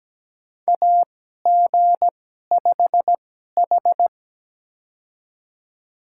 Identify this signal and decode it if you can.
Morse code, heard as a keyed tone, about 17 wpm, spaced as for 12 wpm, 700 Hz. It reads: AG5H
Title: USA Call Signs - 4 Characters - 2x Character Spacing - 17wpm